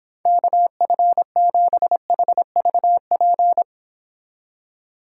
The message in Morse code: KF7HVP

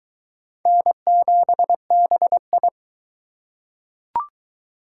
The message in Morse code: N7BI

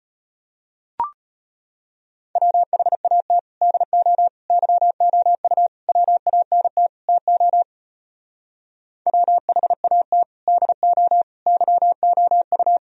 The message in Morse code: WHAT DO YOU WANT TO WHAT DO YOU

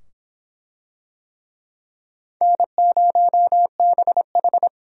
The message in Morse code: N0BH

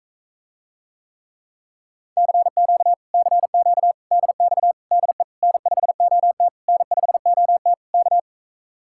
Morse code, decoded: CQ CQ DX DE N5OT N5OT K

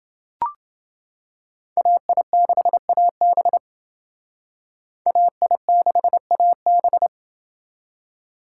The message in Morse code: AI6AB AI6AB

